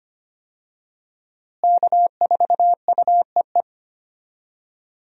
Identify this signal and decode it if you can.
Morse code, heard as a keyed tone, about 25 wpm, 700 Hz. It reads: K4UEE
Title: USA Call Signs - 5 Characters - Repeated 3x - 25wpm